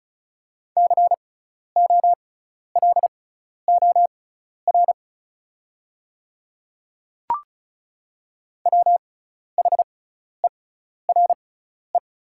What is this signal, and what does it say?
Morse code, heard as a keyed tone, about 36 wpm, spaced as for 12 wpm, 690 Hz. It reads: COLOR WHERE
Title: 6x Character Spacing - Top 500 Words - 35wpm